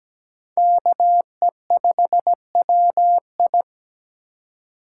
Morse code, heard as a keyed tone, about 17 wpm, 700 Hz. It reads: KE5WI
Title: USA Call Signs - 5 Characters - 3x Character Spacing - 17wpm